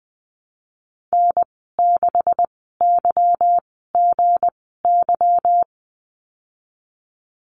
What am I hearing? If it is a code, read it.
Morse code, heard as a keyed tone, about 20 wpm, spaced as for 14 wpm, 700 Hz. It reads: N6YGY
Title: 179 Lesson 29g - Slash - Call Signs 20wpm